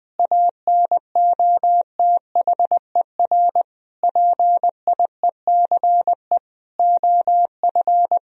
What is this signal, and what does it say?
Morse code, heard as a keyed tone, about 21 wpm, 700 Hz. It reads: ANOTHER PIECE OF
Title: Sentences from Top 1000 Words - Review Part 2 - 20wpm